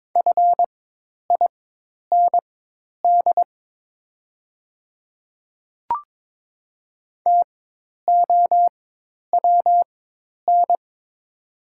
Morse code, heard as a keyed tone, about 22 wpm, spaced as for 10 wpm, 690 Hz. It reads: FIND TOWN
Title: 4x Character Spacing - Top 500 Words - 22wpm